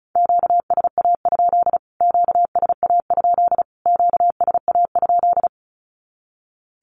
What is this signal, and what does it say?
Morse code, heard as a keyed tone, about 34 wpm, 700 Hz. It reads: QSA? QSA? QSA?
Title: Q-Codes Repeated 3x 35wpm